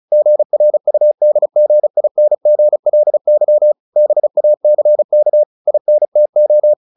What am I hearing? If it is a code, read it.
Morse code, heard as a keyed tone, about 34 wpm, 600 Hz. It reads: GRUDGINGLY BACK INTO
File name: the old man and the sea 2 35wpm